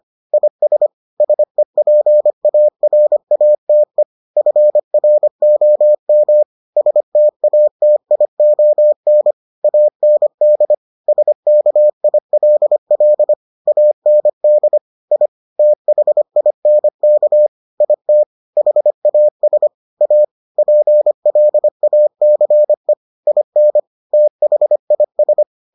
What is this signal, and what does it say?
Morse code, heard as a keyed tone, about 25 wpm, 600 Hz. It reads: IS SEPARATE FROM STATION AND SKILL AND I THINK IT HAS A PLACE IN THIS